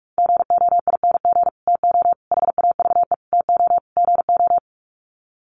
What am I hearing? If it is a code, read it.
Morse code, heard as a keyed tone, about 44 wpm, 700 Hz. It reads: GOING TO HAVE TO GO
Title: Sets of 5 Words Encore 45wpm